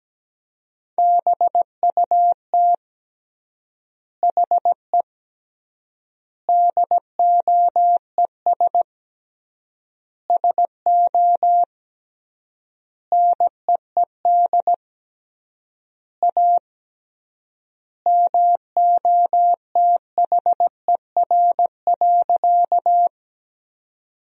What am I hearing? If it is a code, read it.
Morse code, heard as a keyed tone, about 17 wpm, 700 Hz. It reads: BUT HE DOES SO NEED A MOTHER.
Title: 3x Word-Spacing - Sentences from Top 200 Words - 17wpm